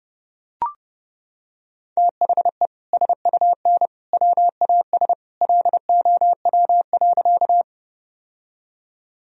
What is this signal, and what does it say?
Morse code, heard as a keyed tone, about 30 wpm, 700 Hz. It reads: THE SUN WAS LOW.
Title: Sentences from Top 1000 Words - Review Part 1 - 30wpm